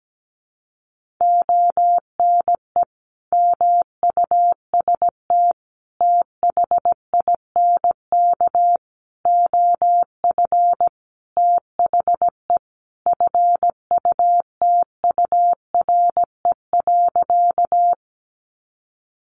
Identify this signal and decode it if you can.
Morse code, heard as a keyed tone, about 17 wpm, 700 Hz. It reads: ONE MUST THINK OF THE FUTURE.
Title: Sentences from Top 700 Words 17wpm